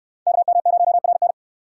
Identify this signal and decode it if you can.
Morse code, heard as a keyed tone, about 64 wpm, 700 Hz. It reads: KM0RN